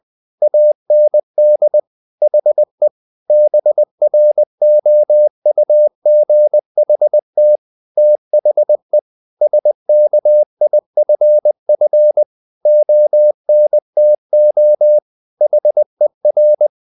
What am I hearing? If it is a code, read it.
Morse code, heard as a keyed tone, about 20 wpm, 600 Hz. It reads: AND HE BROUGHT THE SKIFF ONTO HER
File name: the old man and the sea part 2 20wpm